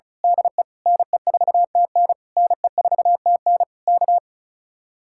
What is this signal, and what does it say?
Morse code, heard as a keyed tone, about 32 wpm, 700 Hz. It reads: DE NE4TN NE4TN K
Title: Speed-Racing Common QSO Phrases Encore - 35wpm